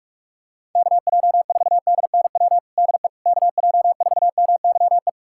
Morse code, heard as a keyed tone, about 41 wpm, 700 Hz. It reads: KJ4DNW DE KJ4MYE
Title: USA Callsign DE Callsign - 6 Characters - 45wpm